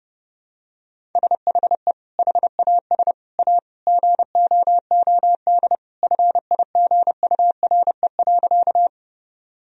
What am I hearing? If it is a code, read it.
Morse code, heard as a keyed tone, about 32 wpm, 700 Hz. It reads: SHE HAS A GOOD FIGURE.
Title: Sentences from Top 1000 Words - Review Part 2 - 30wpm